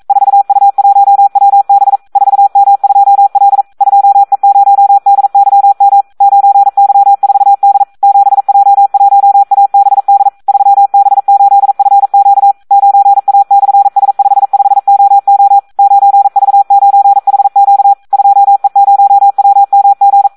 Morse code, heard as a keyed tone, about 42 wpm, 800 Hz. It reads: 4U1WB 4M2L 2E0DYM 9Y4D 7J1ABD 3B8FQ 9A/S55OO 9V9HQ 2E0WMG
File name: Audio practica CW / SD 1 42